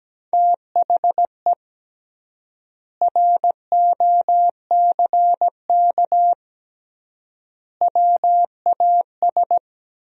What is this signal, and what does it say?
Morse code, heard as a keyed tone, about 17 wpm, 700 Hz. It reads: THE ROCK WAS
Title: 3x Word-Spacing - Sentences from Top 400 Words - 17wpm